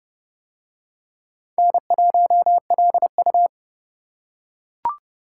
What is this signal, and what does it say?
Morse code, heard as a keyed tone, about 29 wpm, 700 Hz. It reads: N1LU